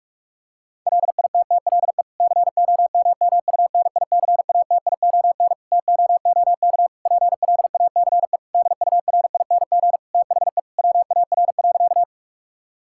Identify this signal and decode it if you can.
Morse code, heard as a keyed tone, about 46 wpm, 700 Hz. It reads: LITTLE COMMUNICATION TOOK PLACE DURING THE WAR.
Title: Sentences from Top 1000 Words - Review Part 6 - 45wpm